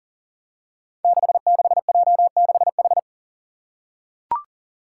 Morse code, heard as a keyed tone, about 40 wpm, 700 Hz. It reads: 66J6H
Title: Five Challenging Letter-Number Combinations 40wpm